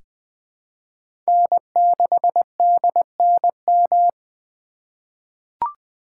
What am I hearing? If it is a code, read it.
Morse code, heard as a keyed tone, about 20 wpm, 700 Hz. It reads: N6DNM
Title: 2023 SOTA Chaser Call Signs - 20wpm